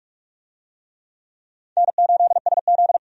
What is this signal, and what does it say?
Morse code, heard as a keyed tone, about 44 wpm, 700 Hz. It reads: N8SZ
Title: USA Call Signs - 4 Characters - 45wpm